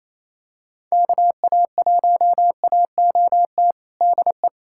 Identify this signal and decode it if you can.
Morse code, heard as a keyed tone, about 27 wpm, 700 Hz. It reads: KA1AOT DE
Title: USA Callsign DE Callsign 28wpm